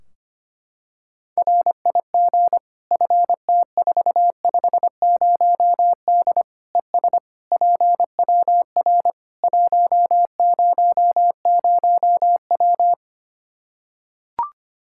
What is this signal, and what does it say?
Morse code, heard as a keyed tone, about 25 wpm, 700 Hz. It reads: RIG FT450D ES PWR 100W
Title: Practice QSO - Exchange 2 - Info - 25wpm